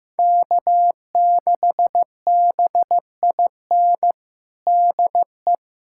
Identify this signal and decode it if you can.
Morse code, heard as a keyed tone, about 15 wpm, 700 Hz. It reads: K6BIN DE
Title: USA Callsign DE Callsign - 5 Characters - 15wpm